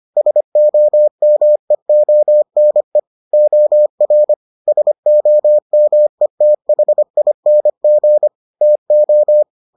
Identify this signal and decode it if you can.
Morse code, heard as a keyed tone, about 25 wpm, 600 Hz. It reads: SOMEONE OR SOMETHING TO